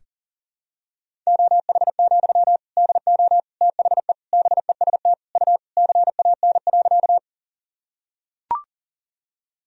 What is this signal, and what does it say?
Morse code, heard as a keyed tone, about 38 wpm, 700 Hz. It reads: OH, DO THE BEST U CAN.
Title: Sentences from Top 500 Words and Abbreviations 40wpm